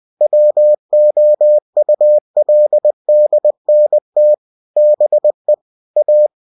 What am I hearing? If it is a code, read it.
Morse code, heard as a keyed tone, about 20 wpm, 600 Hz. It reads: WOULDNT BE A
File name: cw-books / winnie-the-pooh / winnie-20wpm-page-0041